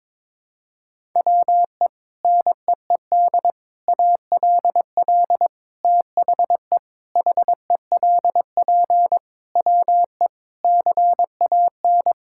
Morse code, heard as a keyed tone, about 22 wpm, 700 Hz. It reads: WE NEED ALL THE HELP WE CAN
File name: practice / 22wpm / Sets of 8 Words - 22wpm